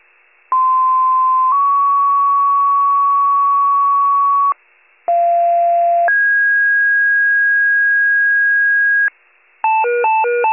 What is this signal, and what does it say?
Two-tone page, heard as a one-second tone, then a three-second tone, about 1040 then 1120 Hz, about 690 then 1650 Hz, with noise. It is heard from 0.5 to 9.1 s.